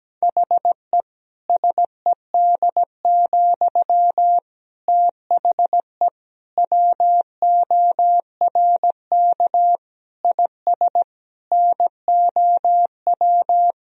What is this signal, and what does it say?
Morse code, heard as a keyed tone, about 17 wpm, 700 Hz. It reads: HE SED, THE WORK IS NOW
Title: Sentences from Top 400 Words and Abbreviations 17wpm